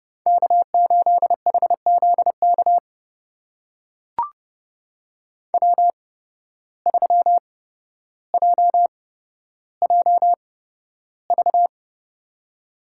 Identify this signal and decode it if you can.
Morse code, heard as a keyed tone, about 31 wpm, spaced as for 13 wpm, 700 Hz. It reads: K8HZK W3JJV
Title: USA Call Signs - 5 Characters - 8x Character Spacing - 30wpm